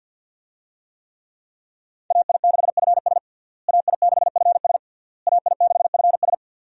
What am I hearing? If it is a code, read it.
Morse code, heard as a keyed tone, about 49 wpm, 700 Hz. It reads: AI6FS AI6FS AI6FS
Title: USA Call Signs - 5 Characters - Repeated 3x - 50wpm